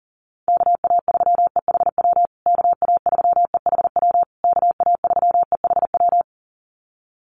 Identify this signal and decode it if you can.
Morse code, heard as a keyed tone, about 37 wpm, 700 Hz. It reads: KA3EHW KA3EHW KA3EHW
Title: USA Call Signs Repeated 3x 40wpm